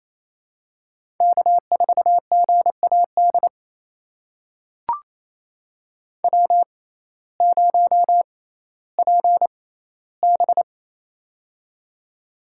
Morse code, heard as a keyed tone, about 28 wpm, spaced as for 15 wpm, 700 Hz. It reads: K4GAD W0PB
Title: USA Call Signs - 6x Character Spacing - 28wpm